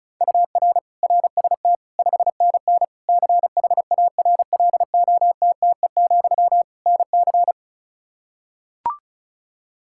Morse code, heard as a keyed tone, about 34 wpm, 700 Hz. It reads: UR RST 5NN CHARLOTTE, NC
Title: K1USN SST QSO Practice 35wpm